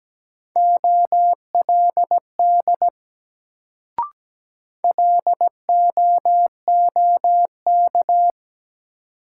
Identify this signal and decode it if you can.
Morse code, heard as a keyed tone, about 17 wpm, 700 Hz. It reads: OLD LOOK
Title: Sets of 2 Words - New Word Drill - 17wpm